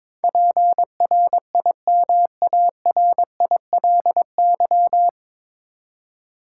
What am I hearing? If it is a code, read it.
Morse code, heard as a keyed tone, about 22 wpm, 700 Hz. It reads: PRIMARILY
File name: 9 Letter Words - First 3 Letters Given - 22wpm